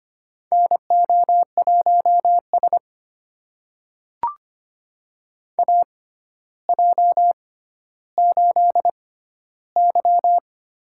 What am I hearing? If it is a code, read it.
Morse code, heard as a keyed tone, about 25 wpm, spaced as for 13 wpm, 700 Hz. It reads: NO1S AJ8Y
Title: USA Call Signs - 4 Characters - 6x Character Spacing - 25wpm